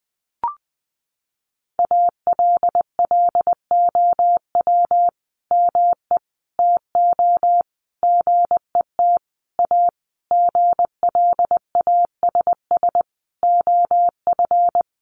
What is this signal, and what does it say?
Morse code, heard as a keyed tone, about 19 wpm, 700 Hz. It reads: ALLOW ME TO GET A GLASS OF
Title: Sentences from Top 1300 Words 20wpm